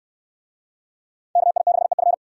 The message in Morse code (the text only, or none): KI6EF